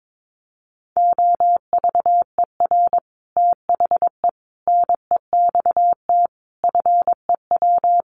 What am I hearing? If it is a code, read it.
Morse code, heard as a keyed tone, about 22 wpm, 700 Hz. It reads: OVER THE NEXT FEW
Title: Sets of 5 Words Encore 22wpm